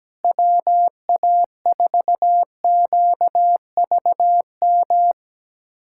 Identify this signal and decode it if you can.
Morse code, heard as a keyed tone, about 17 wpm, 700 Hz. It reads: WA4QVM
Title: USA Callsign DE Callsign - 6 Characters - 17wpm